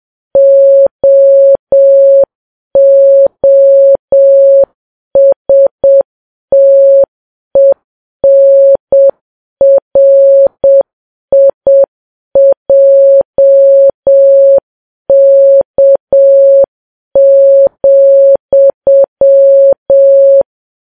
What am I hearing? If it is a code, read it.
Morse code, heard as a keyed tone, about 7 wpm, 550 Hz. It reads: OOSTENRIJK,
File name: Files Web tls / Morse on9fv / morse7kps